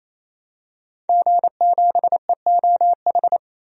Morse code, heard as a keyed tone, about 28 wpm, 700 Hz. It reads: G7EOH
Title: DXCC-100 Call Signs - 5 Characters - 28wpm